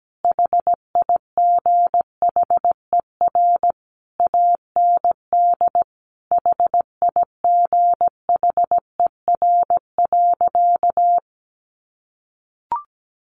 Morse code, heard as a keyed tone, about 17 wpm, 700 Hz. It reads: HIGHER AND HIGHER.
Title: Additional Easy Phrases 17wpm